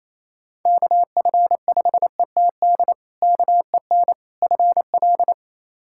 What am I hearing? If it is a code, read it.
Morse code, heard as a keyed tone, about 28 wpm, 700 Hz. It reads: KF5ETD KEN FL
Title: K1USN SST QSO Practice - 28wpm